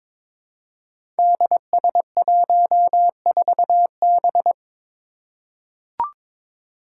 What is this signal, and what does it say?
Morse code, heard as a keyed tone, about 22 wpm, 700 Hz. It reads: DS14B
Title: Five Challenging Letter-Number Combinations 22wpm